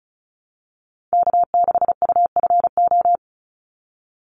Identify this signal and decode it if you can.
Morse code, heard as a keyed tone, about 34 wpm, 700 Hz. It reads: K6UFO
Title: Call Signs - 35wpm